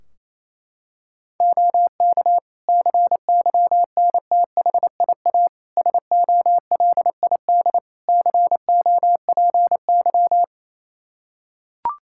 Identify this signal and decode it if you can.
Morse code, heard as a keyed tone, about 27 wpm, 700 Hz. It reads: OK CYNTHIA SOLID COPY
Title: K1USN SST QSO Practice 28wpm